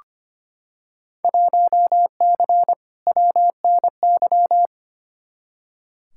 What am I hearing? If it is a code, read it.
Morse code, heard as a keyed tone, about 25 wpm, 700 Hz. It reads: 1C WNY
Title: Field Day Exchanges Encore - 25wpm